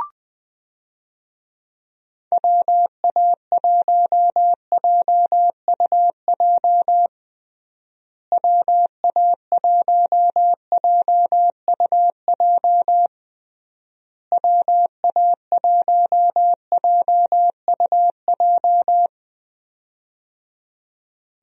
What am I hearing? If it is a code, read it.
Morse code, heard as a keyed tone, about 20 wpm, 700 Hz. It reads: WA1JUJ WA1JUJ WA1JUJ
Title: USA Call Signs - 6 Characters - Repeated 3x - 20wpm